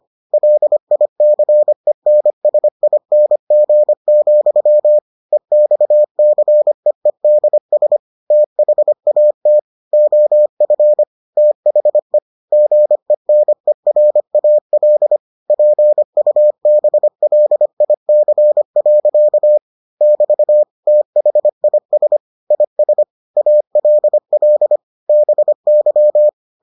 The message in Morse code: LICENSING, EXCEEDS THAT OF THE GENERAL PUBLIC. <BT> THIS IS ALL BY

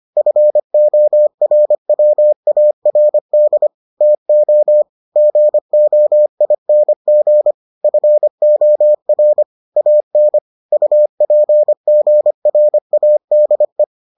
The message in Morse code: FORWARD TO GOING FOR AN UPGRADE

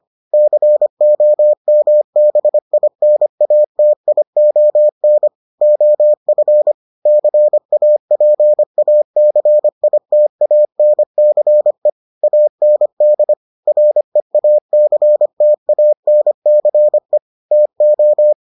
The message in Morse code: COMBINATION OF CAPACITANCE AND REACTANCE TO

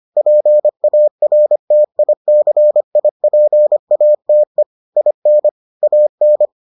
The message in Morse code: PARTICIPATE IN AN

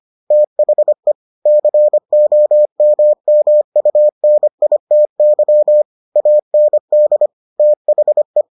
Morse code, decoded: THE COMMUNITY AND THE

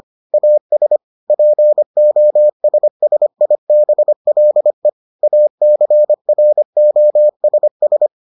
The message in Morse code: AS POSSIBLE ACROSS

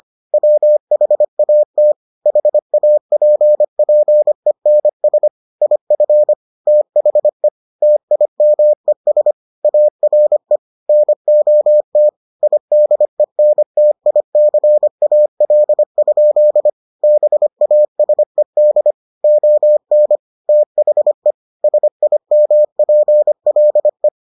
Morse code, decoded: WHAT HAPPENS IF THE TIMES ARE NOT IDENTICAL? BASED ON THE SIMPLE

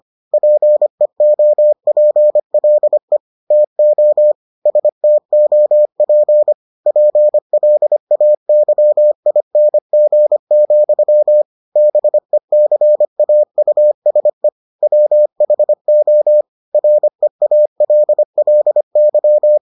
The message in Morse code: PEOPLE TO STOP PLAYING, BECAUSE WHO REALLY

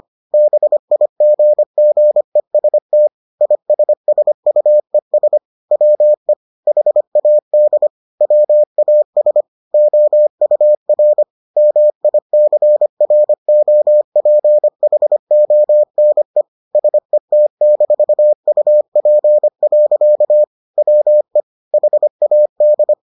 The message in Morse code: BIGGEST ISSUES WE HAD WAS OUR MICROPHONE SET-UP. WE HAD

